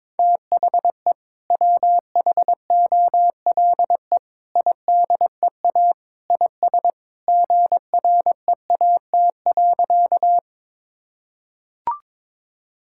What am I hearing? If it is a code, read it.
Morse code, heard as a keyed tone, about 22 wpm, 700 Hz. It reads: THE WHOLE IDEA IS GREAT.
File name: practice / 22wpm / Sentences with 2-Word Phrases - 22wpm